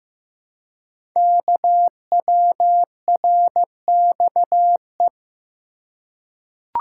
Morse code, heard as a keyed tone, about 15 wpm, 700 Hz. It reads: KWRXE
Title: Code Groups - 5 Letters - 15wpm